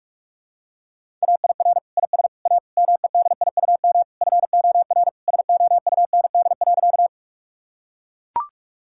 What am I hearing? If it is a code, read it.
Morse code, heard as a keyed tone, about 41 wpm, 700 Hz. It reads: AIR IS A MEDIUM FOR SOUND.
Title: Sentences from Top 1000 Words - Review Part 6 - 45wpm